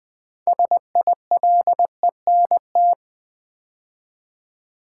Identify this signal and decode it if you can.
Morse code, heard as a keyed tone, about 20 wpm, 700 Hz. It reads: SILENT
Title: 6 Letter Words - First 3 Letters Given - 20wpm